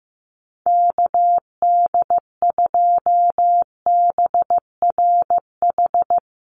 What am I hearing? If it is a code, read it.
Morse code, heard as a keyed tone, about 14 wpm, 700 Hz. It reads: KD2BRH
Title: USA Call Signs 15wpm